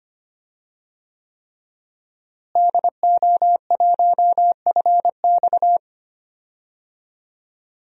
Morse code, heard as a keyed tone, about 25 wpm, 700 Hz. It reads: DO1FX
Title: DXCC-100 Call Signs - 5 Characters - Repeated 3x - 25wpm